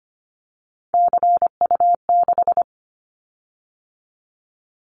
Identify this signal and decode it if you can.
Morse code, heard as a keyed tone, about 25 wpm, 700 Hz. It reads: CU6
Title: Callsign Prefixes for Top 50 DXCC - 25wpm